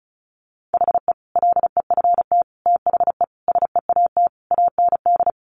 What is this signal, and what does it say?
Morse code, heard as a keyed tone, about 34 wpm, 700 Hz. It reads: HE LEFT THE SEAT AND
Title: Sentences from Top 1500 Words 35wpm